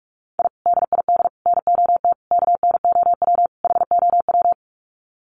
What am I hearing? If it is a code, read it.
Morse code, heard as a keyed tone, about 44 wpm, 700 Hz. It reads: I DID NOT KNOW HOW
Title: Sets of 5 Words Encore 45wpm